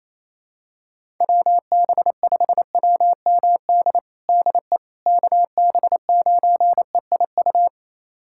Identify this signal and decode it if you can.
Morse code, heard as a keyed tone, about 28 wpm, 700 Hz. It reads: WB5WMD DE KB9EIU
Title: USA Callsign DE Callsign - 6 Characters - 28wpm